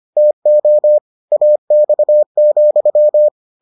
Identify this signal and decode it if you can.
Morse code, heard as a keyed tone, about 25 wpm, 600 Hz. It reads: TO AX,